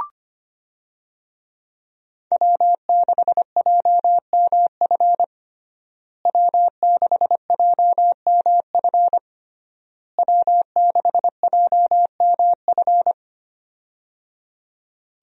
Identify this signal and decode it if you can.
Morse code, heard as a keyed tone, about 25 wpm, 700 Hz. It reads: W6JMF W6JMF W6JMF
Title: USA Call Signs - 5 Characters - Repeated 3x - 25wpm